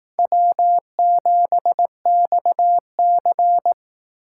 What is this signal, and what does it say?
Morse code, heard as a keyed tone, about 18 wpm, 700 Hz. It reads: W7XC
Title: Speed Racing - USA 1x2 Call Signs 15wpm